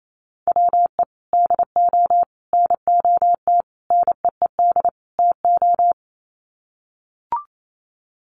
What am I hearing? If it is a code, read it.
Morse code, heard as a keyed tone, about 28 wpm, 700 Hz. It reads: WE DO NOT NEED TO
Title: Sets of 5 Words Encore 28wpm